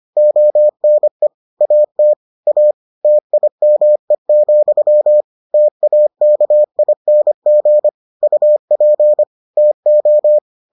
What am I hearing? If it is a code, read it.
Morse code, heard as a keyed tone, about 25 wpm, 600 Hz. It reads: ONE AT A TIME, TAKING UP TO